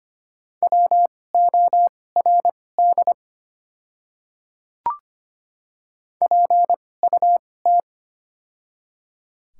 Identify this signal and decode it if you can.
Morse code, heard as a keyed tone, about 25 wpm, spaced as for 18 wpm, 700 Hz. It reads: WORD PUT
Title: 2x Character Spacing - Top 500 Words - 25wpm